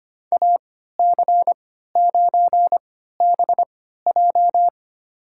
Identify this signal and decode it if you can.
Morse code, heard as a keyed tone, about 25 wpm, spaced as for 14 wpm, 700 Hz. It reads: AC9BJ
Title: USA Call Signs - 5 Characters - 3x Character Spacing - 25wpm